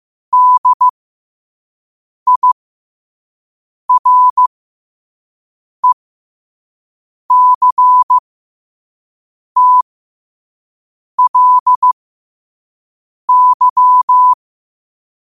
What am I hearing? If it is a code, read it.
Morse code, heard as a keyed tone, about 15 wpm, spaced as for 5 wpm, 1000 Hz. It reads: DIRECTLY
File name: qotdfiles / 5wpm